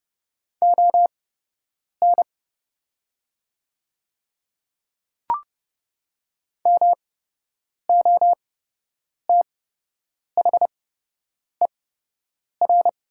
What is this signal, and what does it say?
Morse code, heard as a keyed tone, about 29 wpm, spaced as for 8 wpm, 700 Hz. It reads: ON MOTHER
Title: 8x Character Spacing - Top 300 Words - 30wpm